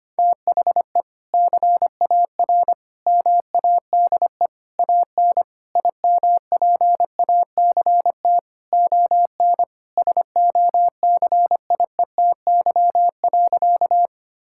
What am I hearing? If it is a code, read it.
Morse code, heard as a keyed tone, about 25 wpm, 700 Hz. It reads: THE CAR MADE AN IMPACT ON SOCIETY.
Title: Sentences from Top 1000 Words - Review Part 6 - 25wpm